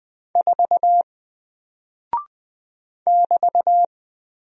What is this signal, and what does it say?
Morse code, heard as a keyed tone, about 20 wpm, 700 Hz. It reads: 4 <BT>
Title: Single Character (Letter:Number:Punctuation:Prosign) Alt 20wpm